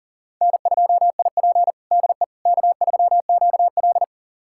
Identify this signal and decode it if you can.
Morse code, heard as a keyed tone, about 40 wpm, 700 Hz. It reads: N2IP DE K3QL
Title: USA Callsign DE Callsign - 4 Characters - 40wpm